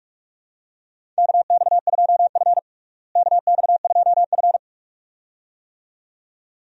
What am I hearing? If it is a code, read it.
Morse code, heard as a keyed tone, about 42 wpm, 700 Hz. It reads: KX2F KX2F
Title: USA Call Signs - 4 Characters - Repeated 2x - 45wpm